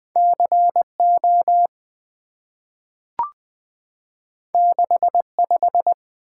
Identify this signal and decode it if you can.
Morse code, heard as a keyed tone, about 20 wpm, 700 Hz. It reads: C O 6 5
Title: Two Letter-Number Combinations - 20wpm